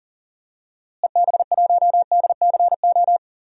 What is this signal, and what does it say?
Morse code, heard as a keyed tone, about 40 wpm, 700 Hz. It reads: EB1DCO
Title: DXCC-100 Call Signs - 6 Characters - Repeated 3x - 40wpm